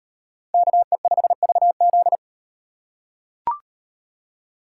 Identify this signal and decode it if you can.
Morse code, heard as a keyed tone, about 38 wpm, 700 Hz. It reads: KE5VZ